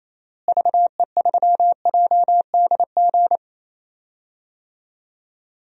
Morse code, heard as a keyed tone, about 28 wpm, 700 Hz. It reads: VE3JDG
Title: DXCC-100 Call Signs - 6 Characters - Repeated 3x - 28wpm